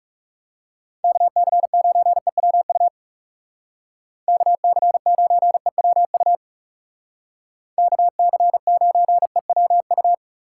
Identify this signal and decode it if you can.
Morse code, heard as a keyed tone, about 36 wpm, 700 Hz. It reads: KC9EWU KC9EWU KC9EWU
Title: Speed Racing - USA Call Signs 30wpm